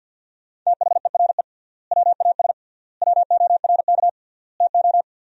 Morse code, heard as a keyed tone, about 49 wpm, 700 Hz. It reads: THERE WAS WORK TO
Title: 3x Word-Spacing - Sentences from Top 200 Words - 50wpm